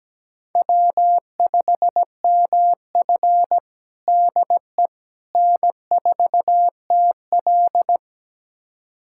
Morse code, heard as a keyed tone, about 17 wpm, 700 Hz. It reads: W5MF DE N4TL
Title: USA Callsign DE Callsign - 4 Characters - 17wpm